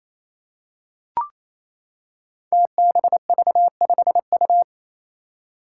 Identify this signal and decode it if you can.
Morse code, heard as a keyed tone, about 28 wpm, 700 Hz. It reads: TBV5U
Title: Five Challenging Letter-Number Combinations 28wpm